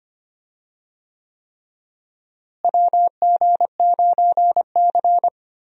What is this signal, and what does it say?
Morse code, heard as a keyed tone, about 25 wpm, 700 Hz. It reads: WG9C
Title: USA Call Signs - 4 Characters - Repeated 3x - 25wpm